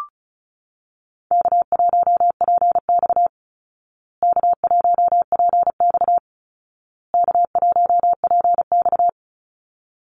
Morse code, heard as a keyed tone, about 34 wpm, 700 Hz. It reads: K1PX K1PX K1PX
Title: Easy Call Signs Repeated 3x - 35wpm